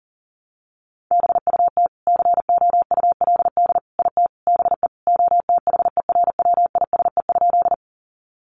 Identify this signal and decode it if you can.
Morse code, heard as a keyed tone, about 40 wpm, 700 Hz. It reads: BUT COULD IT BE OTHERWISE?
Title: Sentences from Top 1200 Words 40wpm